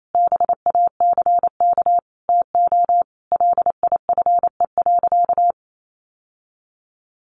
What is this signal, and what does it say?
Morse code, heard as a keyed tone, about 28 wpm, 700 Hz. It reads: BACK TO LIFE.
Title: Sentences from Top 600 Words 28wpm